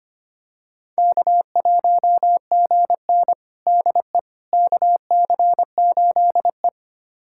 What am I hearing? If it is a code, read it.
Morse code, heard as a keyed tone, about 25 wpm, 700 Hz. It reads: K1GN DE KC8E
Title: USA Callsign DE Callsign - 4 Characters - 25wpm